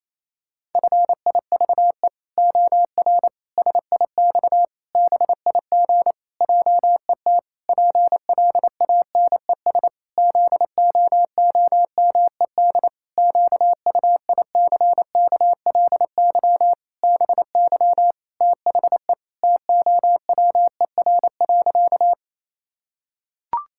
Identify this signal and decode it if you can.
Morse code, heard as a keyed tone, about 27 wpm, 700 Hz. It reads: FIVE OR SIX BIG JET PLANES ZOOMED QUICKLY BY THE TOWER.
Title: Pangram Challenge v2 - 28wpm